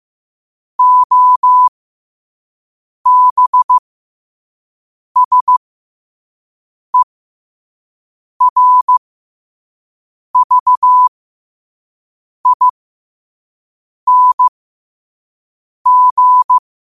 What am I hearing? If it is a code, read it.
Morse code, heard as a keyed tone, about 15 wpm, spaced as for 5 wpm, 1000 Hz. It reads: OBSERVING
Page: New quotes every day in morse code at 5 Words per minute.